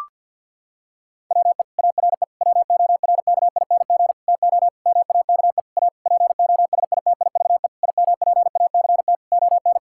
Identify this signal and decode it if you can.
Morse code, heard as a keyed tone, about 50 wpm, 700 Hz. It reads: WE ARE WORKING TO MAKE A POSITIVE IMPACT ON
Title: Sentences from Top 1000 Words - Review Part 6 - 50wpm